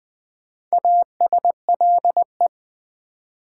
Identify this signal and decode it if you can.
Morse code, heard as a keyed tone, about 20 wpm, 700 Hz. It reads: ASLE
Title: Build A Word - 2 Letters At A Time - 6-Letter Words - 20wpm